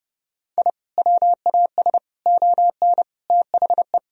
Morse code, heard as a keyed tone, about 30 wpm, 700 Hz. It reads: I WAS ON THE
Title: Sentences from Top 1000 Words - Review Part 2 - 30wpm